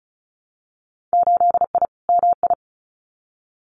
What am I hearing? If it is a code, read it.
Morse code, heard as a keyed tone, about 36 wpm, 700 Hz. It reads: 8I MI
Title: Winter Field Day Exchanges Encore 35wpm